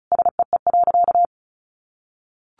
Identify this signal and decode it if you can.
Morse code, heard as a keyed tone, about 34 wpm, 700 Hz. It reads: SEE.
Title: Sentences from Top 200 Words 35wpm